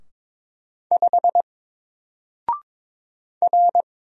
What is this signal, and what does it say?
Morse code, heard as a keyed tone, about 22 wpm, 700 Hz. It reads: 5 R
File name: Single-Letter-Number-22wpm